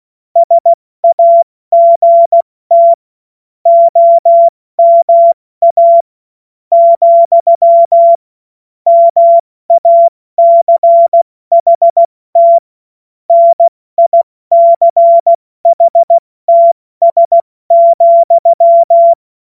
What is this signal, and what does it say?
Morse code, heard as a keyed tone, about 16 wpm, spaced as for 13 wpm, 680 Hz. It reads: SAGT OMA, MACHT NICHTS,
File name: Morsekurs / Witze / Witz 28. September 2021, 70 F80